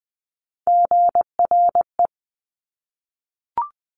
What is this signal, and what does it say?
Morse code, heard as a keyed tone, about 20 wpm, 700 Hz. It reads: GRE
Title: Top Three Letter Combinations in Words 20wpm